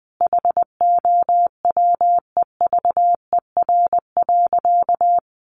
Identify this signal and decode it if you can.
Morse code, heard as a keyed tone, about 20 wpm, 700 Hz. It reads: HOWEVER.
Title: Sentences from Top 1200 Words 20wpm